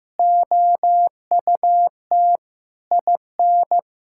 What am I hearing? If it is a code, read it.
Morse code, heard as a keyed tone, about 15 wpm, 700 Hz. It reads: OUT IN
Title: Sentences from Top 1000 Words - Review Part 5 - 15wpm